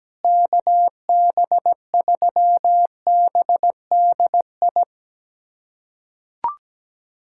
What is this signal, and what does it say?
Morse code, heard as a keyed tone, about 17 wpm, 700 Hz. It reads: KB3BDI
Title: USA Callsign DE Callsign 17wpm